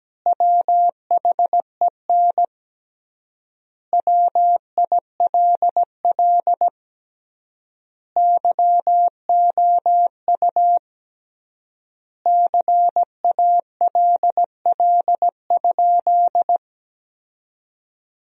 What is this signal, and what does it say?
Morse code, heard as a keyed tone, about 17 wpm, 700 Hz. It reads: WHEN WILL YOU CALL?
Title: 3x Word-Spacing - Sentences from Top 100 Words - 17wpm